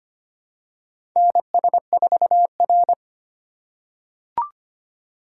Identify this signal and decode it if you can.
Morse code, heard as a keyed tone, about 25 wpm, 700 Hz. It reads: NS4R
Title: USA Call Signs - 4x Character Spacing - 25wpm